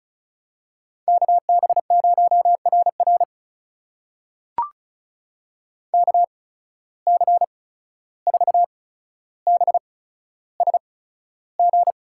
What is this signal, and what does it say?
Morse code, heard as a keyed tone, about 36 wpm, spaced as for 14 wpm, 700 Hz. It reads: KB0RR KC4BSG
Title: USA Call Signs - 8x Character Spacing - 35wpm